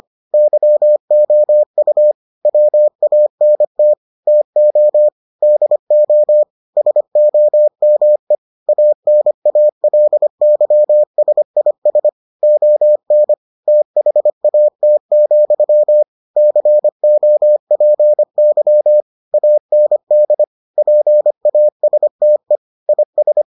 Morse code, heard as a keyed tone, about 25 wpm, 600 Hz. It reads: YOU WANT TO DO SOME ANALYSIS ON THAT, COPY AND PASTE IS